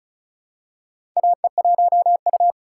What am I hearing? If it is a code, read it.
Morse code, heard as a keyed tone, about 36 wpm, 700 Hz. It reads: AE1U
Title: USA Call Signs - 4 Characters - 4x Character Spacing - 35wpm